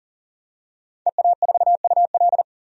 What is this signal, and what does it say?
Morse code, heard as a keyed tone, about 40 wpm, 700 Hz. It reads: EA4UL
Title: DXCC-100 Call Signs - 5 Characters - 40wpm